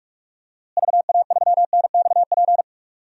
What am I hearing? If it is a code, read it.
Morse code, heard as a keyed tone, about 44 wpm, 700 Hz. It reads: VA3NXP
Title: DXCC-100 Call Signs - 6 Characters - Repeated 2x - 45wpm